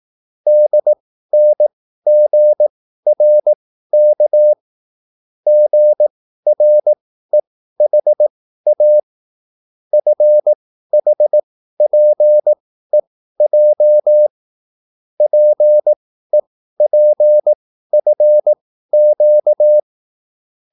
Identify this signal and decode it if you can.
Morse code, heard as a keyed tone, about 18 wpm, spaced as for 13 wpm, 600 Hz. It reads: DNGRK GREHA FHPEJ PEPFQ
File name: QSL / CW lesson